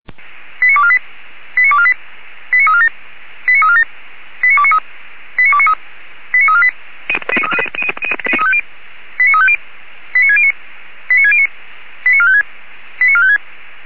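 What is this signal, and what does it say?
Five-tone selective call (ZVEI), heard as quick runs of five tones, tones of about 70 ms. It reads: 89248 89248 89347 89347 89293 89293 89348 89350 89609 89609 89456 89456
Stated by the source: ZVEI five-tone call 89406 received on an SDR.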